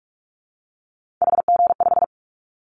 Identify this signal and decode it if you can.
Morse code, heard as a keyed tone, about 46 wpm, 700 Hz. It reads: HG5
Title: Callsign Prefixes for Top 25 DXCC - 45wpm